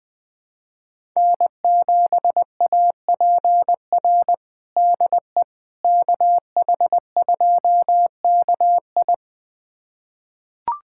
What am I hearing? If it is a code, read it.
Morse code, heard as a keyed tone, about 21 wpm, 700 Hz. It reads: N7APR DE KH2KI
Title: USA Callsign DE Callsign - 5 Characters - 20wpm